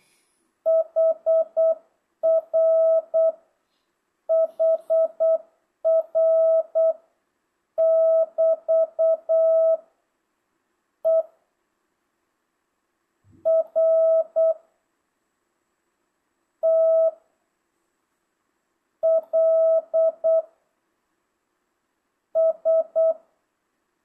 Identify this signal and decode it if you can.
Morse code, heard as a keyed tone, about 8 wpm, 650 Hz. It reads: HR HR <BT> E R T L S
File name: jh8cbh / 2025cw